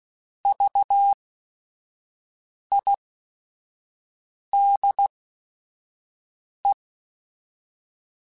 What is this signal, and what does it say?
Morse code, heard as a keyed tone, about 16 wpm, spaced as for 5 wpm, 800 Hz. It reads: VIDE
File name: Technical Subjects / Technical Subjects-mp3s / Sw Pwr Sup @5